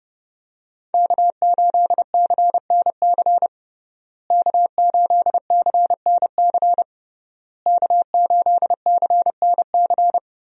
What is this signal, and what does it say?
Morse code, heard as a keyed tone, about 30 wpm, 700 Hz. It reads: K8CNC K8CNC K8CNC
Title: USA Call Signs - 5 Characters - Repeated 3x - 30wpm